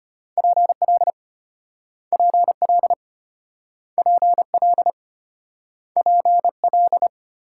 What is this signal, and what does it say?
Morse code, heard as a keyed tone, about 34 wpm, 700 Hz. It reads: PL PL PL PL